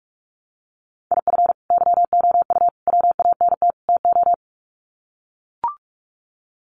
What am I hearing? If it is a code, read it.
Morse code, heard as a keyed tone, about 44 wpm, 700 Hz. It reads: IF YOU WANT TO
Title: Sets of 4 Words 45wpm